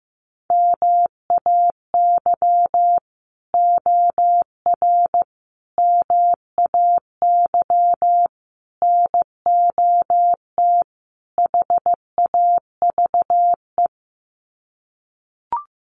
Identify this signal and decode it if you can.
Morse code, heard as a keyed tone, about 15 wpm, 700 Hz. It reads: MAY OR MAY NOT HAVE
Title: Sets of 5 Words Encore2 15wpm